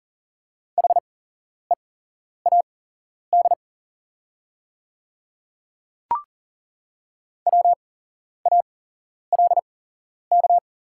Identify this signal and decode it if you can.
Morse code, heard as a keyed tone, about 40 wpm, spaced as for 11 wpm, 700 Hz. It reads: HEAD WALK
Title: 8x Character Spacing - Top 300 Words - 40wpm